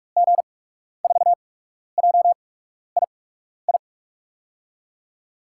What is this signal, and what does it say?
Morse code, heard as a keyed tone, about 47 wpm, spaced as for 12 wpm, 700 Hz. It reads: G4JII
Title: USA Call Signs - 6 Characters - 8x Character Spacing - 45wpm